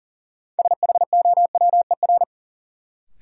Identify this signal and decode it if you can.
Morse code, heard as a keyed tone, about 40 wpm, 700 Hz. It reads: SHOWER
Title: Sentences from Top 1600 Words - New Word Drill Alt - 40wpm